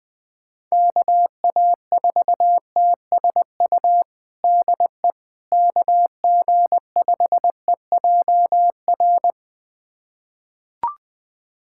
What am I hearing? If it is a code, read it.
Morse code, heard as a keyed tone, about 20 wpm, 700 Hz. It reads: KA4TSU DE KG5EJR
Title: USA Callsign DE Callsign - 6 Characters - 20wpm